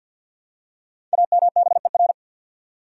Morse code, heard as a keyed tone, about 51 wpm, 700 Hz. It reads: AMBER
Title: USA Common Names - Max 5 Letters - 50wpm